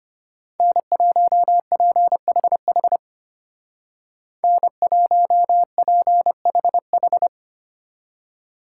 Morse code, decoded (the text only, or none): N1PHH N1PHH